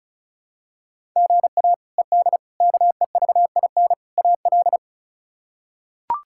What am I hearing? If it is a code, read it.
Morse code, heard as a keyed tone, about 36 wpm, 700 Hz. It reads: GA ED KEVIN AL
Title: K1USN SST QSO Practice - 35wpm